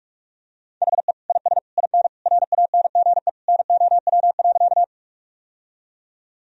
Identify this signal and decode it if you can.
Morse code, heard as a keyed tone, about 46 wpm, 700 Hz. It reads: HE IS IN RANGE NOW.
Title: Sentences from Top 1000 Words - Review Part 4 - 45wpm